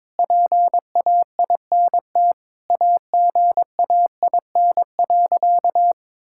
Morse code, heard as a keyed tone, about 22 wpm, 700 Hz. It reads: PAINT AGAIN.
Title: Sentences from Top 1000 Words - Review Part 6 - 22wpm